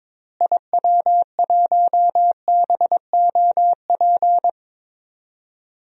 Morse code, heard as a keyed tone, about 22 wpm, 700 Hz. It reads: IW1BOP
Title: DXCC-100 Call Signs - 6 Characters - Repeated 2x - 22wpm